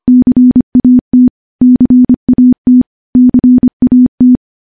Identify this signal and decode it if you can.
Morse code, heard as a keyed tone, about 25 wpm, 250 Hz. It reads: CAT CAT CAT
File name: morse / morse-training-25WPM